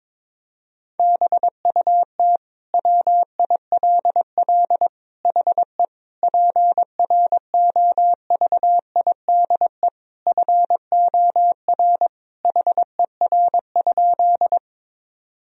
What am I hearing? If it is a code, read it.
Morse code, heard as a keyed tone, about 21 wpm, 700 Hz. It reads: BUT WILL HE PROVIDE FOR HER?
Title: Sentences from Top 1000 Words - Review Part 3 - 22wpm